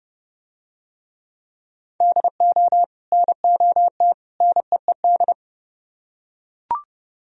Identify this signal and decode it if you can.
Morse code, heard as a keyed tone, about 29 wpm, 700 Hz. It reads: DO NOT NEED